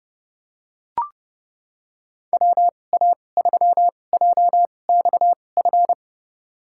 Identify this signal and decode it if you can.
Morse code, heard as a keyed tone, about 32 wpm, spaced as for 22 wpm, 700 Hz. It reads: WA3JXF
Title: USA Call Signs - 2x Character Spacing - 30wpm